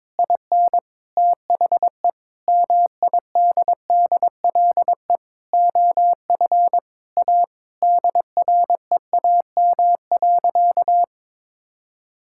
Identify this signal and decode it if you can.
Morse code, heard as a keyed tone, about 21 wpm, 700 Hz. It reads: IN THE MIDDLE OF A DREAM.
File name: practice / 22wpm / Sentences with 4-Word Phrases - 22wpm